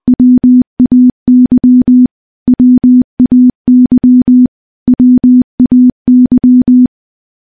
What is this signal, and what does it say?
Morse code, heard as a keyed tone, about 20 wpm, 250 Hz. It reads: WAY WAY WAY
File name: morse-training-20WPM